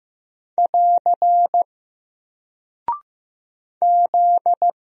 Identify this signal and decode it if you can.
Morse code, heard as a keyed tone, about 15 wpm, 700 Hz. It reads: <AR> Z
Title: Single Character (Letter:Number:Punctuation:Prosign) Alt 15wpm